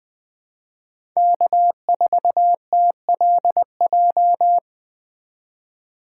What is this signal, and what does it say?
Morse code, heard as a keyed tone, about 20 wpm, 700 Hz. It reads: K4TLJ
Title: Speed Racing - USA Call Signs 17wpm